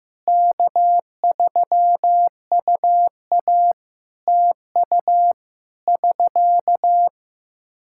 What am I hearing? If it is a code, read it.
Morse code, heard as a keyed tone, about 15 wpm, 700 Hz. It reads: K3UA TU <SK>
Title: Practice QSO - Exchange 4 - Ending - 15wpm